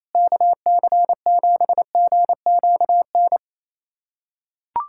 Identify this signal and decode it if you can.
Morse code, heard as a keyed tone, about 27 wpm, 700 Hz. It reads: KC7GQN
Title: USA Callsign DE Callsign 28wpm